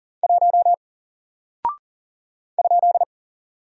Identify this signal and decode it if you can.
Morse code, heard as a keyed tone, about 39 wpm, 700 Hz. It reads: 1 ?
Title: Single Character (Letter:Number:Punctuation:Prosign) 40wpm